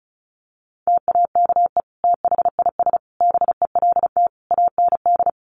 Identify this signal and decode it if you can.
Morse code, heard as a keyed tone, about 34 wpm, 700 Hz. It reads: TAKE THIS BELT AND
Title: Sentences from Top 1400 Words 35wpm